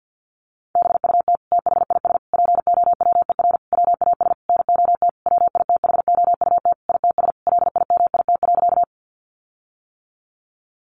Morse code, heard as a keyed tone, about 50 wpm, 700 Hz. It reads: BUT THIS POWER WAS NOT WITHOUT ITS LIMIT.
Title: Sentences from Top 1500 Words 50wpm